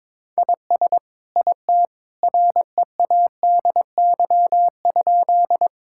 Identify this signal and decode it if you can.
Morse code, heard as a keyed tone, about 22 wpm, 700 Hz. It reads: IS IT READY?
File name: practice / 22wpm / Sentences with 2-Word Phrases - 22wpm